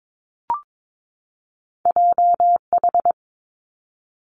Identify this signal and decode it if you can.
Morse code, heard as a keyed tone, about 22 wpm, 700 Hz. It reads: J H
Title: Two Challenging Letter-Number Combinations 22wpm